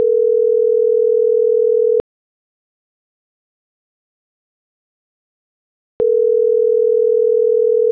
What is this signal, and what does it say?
Ringback tone (North America), 440 + 480 Hz.